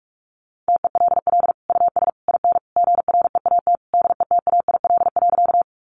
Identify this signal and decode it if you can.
Morse code, heard as a keyed tone, about 44 wpm, 700 Hz. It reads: TELL US IN GREAT DETAIL.
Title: Sentences from Top 1500 Words 45wpm